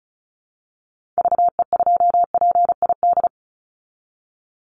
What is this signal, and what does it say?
Morse code, heard as a keyed tone, about 34 wpm, 700 Hz. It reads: VE2PID
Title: Call Signs Repeated 3x - 35wpm